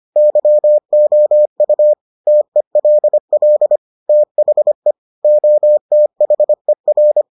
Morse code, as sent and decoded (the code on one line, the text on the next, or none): -.-- --- ..- / - . .-.. .-.. / - .... / --- - .... .-.
YOU TELL THE OTHER